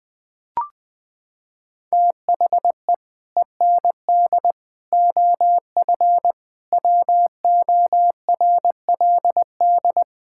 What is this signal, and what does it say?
Morse code, heard as a keyed tone, about 21 wpm, 700 Hz. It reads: THE END OF WORLD
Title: Sets of 6 Words - 20wpm